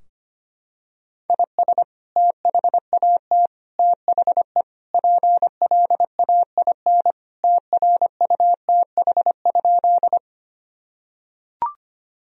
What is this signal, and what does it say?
Morse code, heard as a keyed tone, about 25 wpm, 700 Hz. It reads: IS THAT THE PLAIN TRUTH?
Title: Sentences from Top 1000 Words - Review Part 2 - 25wpm